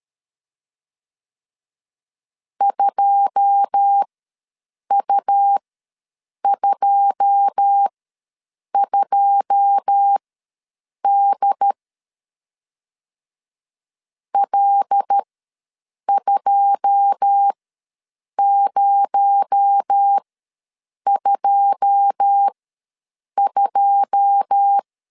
Morse code, heard as a keyed tone, about 13 wpm, spaced as for 7 wpm, 800 Hz. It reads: 2U22D L2022
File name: LEZIONI / 9 2 ws 12wpm 3406